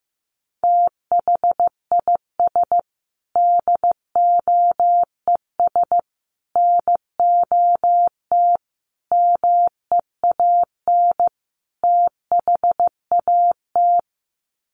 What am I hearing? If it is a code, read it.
Morse code, heard as a keyed tone, about 15 wpm, 700 Hz. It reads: THIS DOES NOT MEAN THAT
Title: Sets of 5 Words Encore2 15wpm